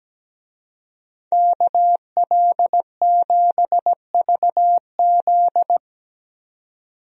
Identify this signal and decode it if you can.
Morse code, heard as a keyed tone, about 17 wpm, 700 Hz. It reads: KL7VZ
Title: DXCC-100 Call Signs - 5 Characters - 17wpm